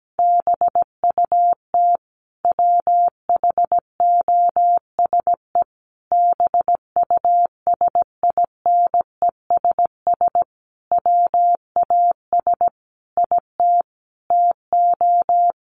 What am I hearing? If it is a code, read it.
Morse code, heard as a keyed tone, about 17 wpm, 700 Hz. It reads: BUT WHOSE BUSINESS WAS IT TO
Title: Sentences from Top 1500 Words 17wpm